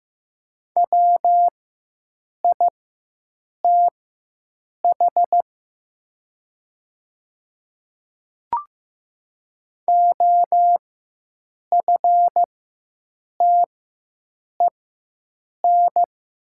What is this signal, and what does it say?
Morse code, heard as a keyed tone, about 15 wpm, spaced as for 7 wpm, 690 Hz. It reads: WITH OFTEN
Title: 4x Character Spacing - Top 500 Words - 15wpm